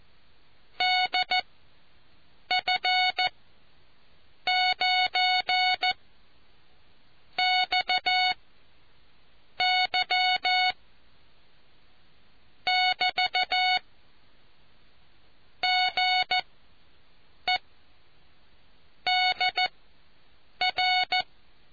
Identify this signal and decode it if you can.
Morse code, heard as a keyed tone, about 14 wpm, spaced as for 6 wpm, 740 Hz. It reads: DF9 XY <BT> GE DR